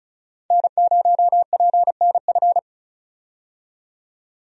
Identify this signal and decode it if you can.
Morse code, heard as a keyed tone, about 36 wpm, 700 Hz. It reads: N0PNF
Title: USA Call Signs - 5 Characters - 35wpm